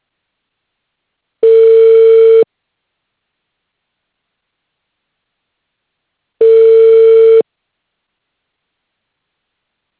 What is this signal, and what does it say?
Ringback tone (China, 450 Hz).